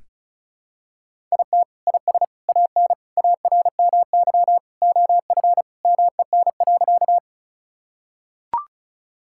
Morse code, heard as a keyed tone, about 34 wpm, 700 Hz. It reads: IT IS AN ARMY OF MEN.
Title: Sentences from Top 1000 Words - Review Part 5 - 35wpm